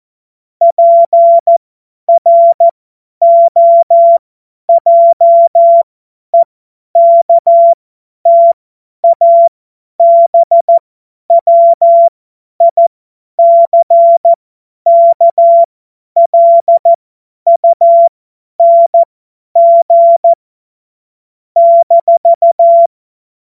Morse code, heard as a keyed tone, about 14 wpm, spaced as for 10 wpm, 680 Hz. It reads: PROJEKTABWICKLUNG -
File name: Morsekurs / Witze / Witz 09. Dezember 2025 50 F70